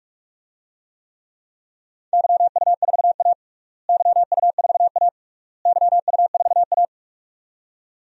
Morse code, heard as a keyed tone, about 47 wpm, 690 Hz. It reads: YU4A YU4A YU4A
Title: DXCC-100 Call Signs - 4 Characters - Repeated 3x - 45wpm